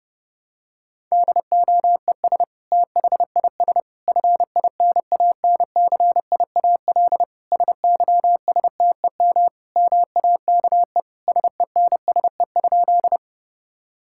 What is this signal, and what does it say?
Morse code, heard as a keyed tone, about 30 wpm, 700 Hz. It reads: DOES THIS FINANCIAL SYSTEM MAKE SENSE?
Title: Sentences from Top 1000 Words - Review Part 5 - 30wpm